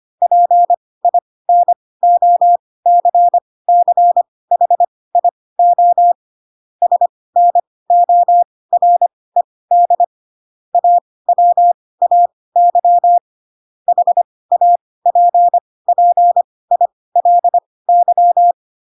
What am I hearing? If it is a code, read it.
Morse code, heard as a keyed tone, about 25 wpm, spaced as for 18 wpm, 700 Hz. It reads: PINOCCHIO SNORED AWAY HAPPILY
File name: Pinocchio / Pinocchio ch6 12 25 ik1hge